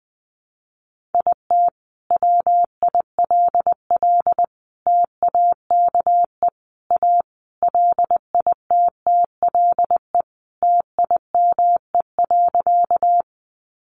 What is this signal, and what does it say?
Morse code, heard as a keyed tone, about 20 wpm, 700 Hz. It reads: IT WILL TAKE A LITTLE TIME.
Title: Sentences from Top 200 Words 20wpm